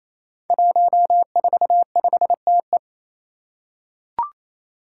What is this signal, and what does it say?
Morse code, heard as a keyed tone, about 27 wpm, 700 Hz. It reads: 145TE
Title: Five Challenging Letter-Number Combinations 28wpm